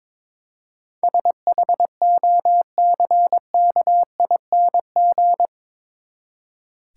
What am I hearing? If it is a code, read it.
Morse code, heard as a keyed tone, about 22 wpm, 700 Hz. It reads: SHOCKING
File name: practice / 22wpm / Eight Letter Words - 22wpm